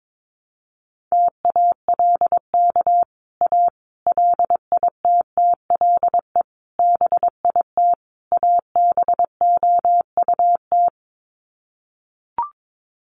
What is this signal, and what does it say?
Morse code, heard as a keyed tone, about 22 wpm, 700 Hz. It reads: TALK A LITTLE BIT ABOUT
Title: Sets of 5 Words Encore 22wpm